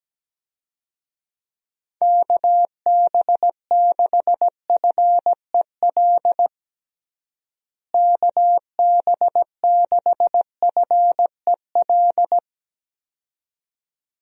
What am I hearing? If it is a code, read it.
Morse code, heard as a keyed tone, about 17 wpm, 700 Hz. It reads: KB6FEL KB6FEL
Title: USA Call Signs - 6 Characters - Repeated 2x - 17wpm